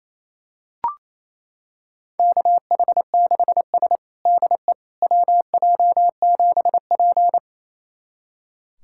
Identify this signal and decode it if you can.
Morse code, heard as a keyed tone, about 28 wpm, 700 Hz. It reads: KH6S DE WJ7P
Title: USA Callsign DE Callsign - 4 Characters - 28wpm